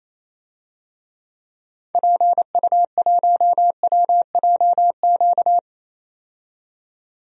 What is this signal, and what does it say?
Morse code, heard as a keyed tone, about 28 wpm, 700 Hz. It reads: PU1WJQ
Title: DXCC-100 Call Signs - 6 Characters - 28wpm